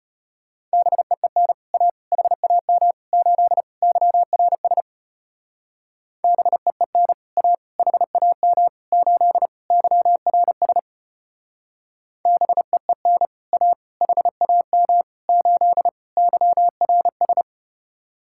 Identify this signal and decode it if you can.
Morse code, heard as a keyed tone, about 33 wpm, 700 Hz. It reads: BEEN A HAM 8 YRS BEEN A HAM 8 YRS BEEN A HAM 8 YRS